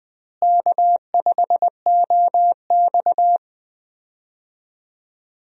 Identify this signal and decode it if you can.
Morse code, heard as a keyed tone, about 20 wpm, 700 Hz. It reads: K5OX
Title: USA Call Signs - 4 Characters - Repeated 3x - 20wpm